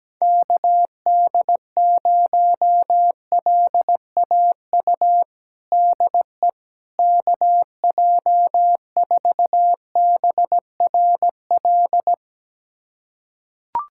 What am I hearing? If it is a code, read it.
Morse code, heard as a keyed tone, about 17 wpm, 700 Hz. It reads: KD0LAU DE KJ4BRL
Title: USA Callsign DE Callsign 17wpm